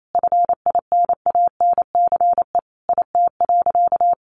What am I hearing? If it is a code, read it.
Morse code, heard as a keyed tone, about 28 wpm, 700 Hz. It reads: FINANCE IT.
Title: Sentences from Top 1500 Words 28wpm